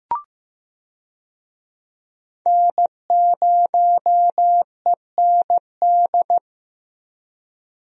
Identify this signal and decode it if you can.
Morse code, heard as a keyed tone, about 15 wpm, 700 Hz. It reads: N0END
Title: USA Call Signs - 5 Characters - Repeated 3x - 15wpm